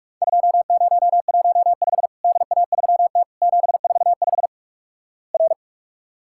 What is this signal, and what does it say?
Morse code, heard as a keyed tone, about 44 wpm, 700 Hz. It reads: 2015 DA3T 745 R
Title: Worked All Europe DX Contest - QTC Exchange - 45wpm